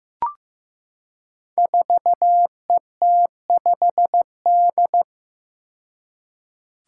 Morse code, heard as a keyed tone, about 15 wpm, 700 Hz. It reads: 4ET5D
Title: Five Challenging Letter-Number Combinations 15wpm